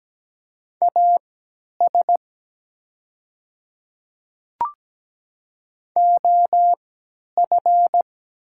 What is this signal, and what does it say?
Morse code, heard as a keyed tone, about 17 wpm, spaced as for 10 wpm, 700 Hz. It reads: AS OF
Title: 3x Character Spacing - Top 100 Words - 17wpm